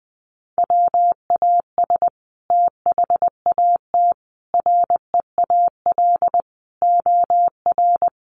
Morse code, heard as a keyed tone, about 20 wpm, 700 Hz. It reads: WAS THAT REAL OR
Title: Sentences from Top 600 Words 20wpm